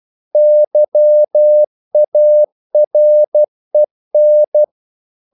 Morse code, heard as a keyed tone, about 12 wpm, 600 Hz. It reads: YAREN
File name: CW / Practice / NationalCapitals12